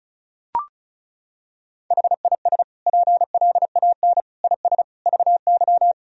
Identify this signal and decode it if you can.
Morse code, heard as a keyed tone, about 36 wpm, 700 Hz. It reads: HIS PLAN IS VY
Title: Sentences from Top 500 Words and Abbreviations 35wpm